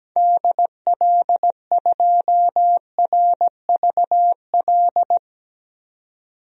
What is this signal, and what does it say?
Morse code, heard as a keyed tone, about 17 wpm, 700 Hz. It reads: DL2RVL
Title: DXCC-100 Call Signs - 6 Characters - 17wpm